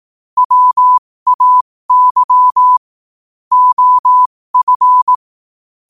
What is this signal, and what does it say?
Morse code, heard as a keyed tone, about 18 wpm, spaced as for 15 wpm, 1000 Hz. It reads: WAY OF
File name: 15wpm